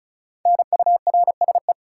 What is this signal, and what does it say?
Morse code, heard as a keyed tone, about 36 wpm, 700 Hz. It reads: NURSE
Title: Five Letter Words - 35wpm